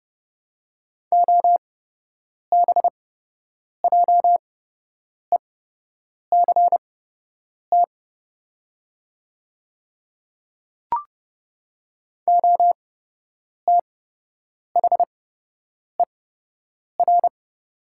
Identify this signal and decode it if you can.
Morse code, heard as a keyed tone, about 30 wpm, spaced as for 8 wpm, 700 Hz. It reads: OBJECT OTHER
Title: 8x Character Spacing - Top 500 Words - 30wpm